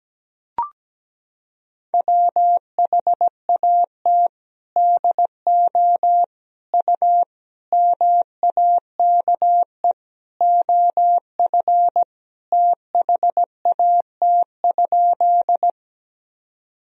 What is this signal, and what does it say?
Morse code, heard as a keyed tone, about 17 wpm, 700 Hz. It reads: WHAT DO U MAKE OF THAT?
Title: Sentences from Top 100 Words and Abbreviations 17wpm